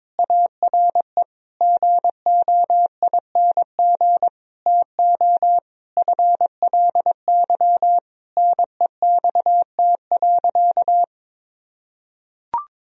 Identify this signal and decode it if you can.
Morse code, heard as a keyed tone, about 22 wpm, 700 Hz. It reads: ARE GOING TO FLY NEXT.
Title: Sentences from Top 1000 Words - Review Part 2 - 22wpm